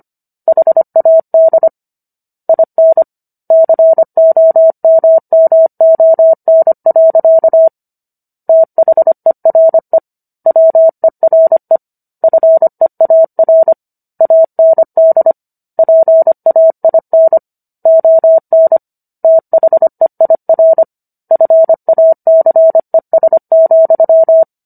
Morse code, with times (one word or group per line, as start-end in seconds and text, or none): 0.5-1.7 HAD
2.5-3.0 IN
3.5-7.7 COMMON.
8.5-10.0 THERE
10.5-11.8 WERE
12.2-13.7 FEAR
14.2-15.3 AND
15.8-17.4 PAIN
17.9-18.8 ON
19.2-20.8 THEIR
21.3-24.4 FACES,